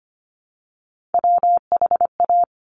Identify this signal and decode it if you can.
Morse code, heard as a keyed tone, about 25 wpm, 700 Hz. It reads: WHA
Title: Top Three Letter Combinations in Words 25wpm